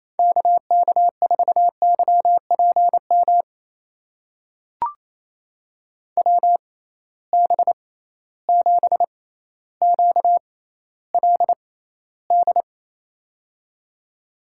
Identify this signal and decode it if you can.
Morse code, heard as a keyed tone, about 28 wpm, spaced as for 14 wpm, 700 Hz. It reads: KK4YPM WB7QLD
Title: USA Call Signs - 6 Characters - 6x Character Spacing - 28wpm